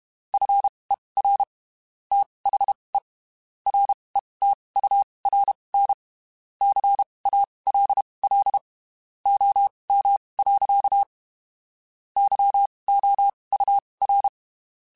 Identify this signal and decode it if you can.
Morse code, heard as a keyed tone, about 32 wpm, spaced as for 23 wpm, 800 Hz. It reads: FER THE RETURN CALL OM. YOUR
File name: (spaced) / QSOs Spaced-mp3s / kc7sac @30 sp